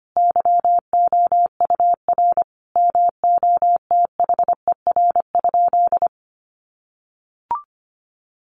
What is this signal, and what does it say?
Morse code, heard as a keyed tone, about 25 wpm, 700 Hz. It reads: YOUR MOTHER?
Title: Sentences from Top 200 Words 25wpm